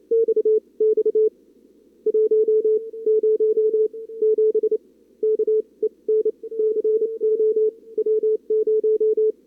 Morse code, heard as a keyed tone, about 28 wpm, spaced as for 21 wpm, 440 Hz. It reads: XX 107 KEN CO W0
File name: CWSprint / Feb2022 / k5tr-feb-na-sprint-cw-k5ot -0000